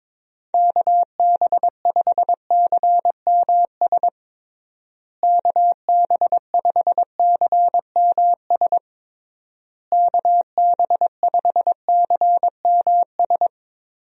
Morse code, heard as a keyed tone, about 22 wpm, 700 Hz. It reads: KB5CMS KB5CMS KB5CMS
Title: USA Call Signs - 6 Characters - Repeated 3x - 22wpm